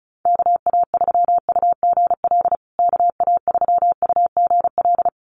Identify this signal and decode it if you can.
Morse code, heard as a keyed tone, about 34 wpm, 700 Hz. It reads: KA3UGL KA3UGL
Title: USA Call Signs Repeated 2x 35wpm